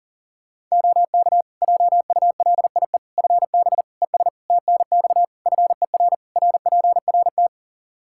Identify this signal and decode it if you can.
Morse code, heard as a keyed tone, about 40 wpm, 700 Hz. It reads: OK JULIE FB ES TNX FER RPRT
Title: Practice QSO - Exchange 2 - Info - 40wpm